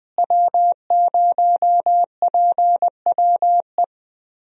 Morse code, heard as a keyed tone, about 21 wpm, 700 Hz. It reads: W0PWE
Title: USA Call Signs - 5 Characters - Repeated 3x - 20wpm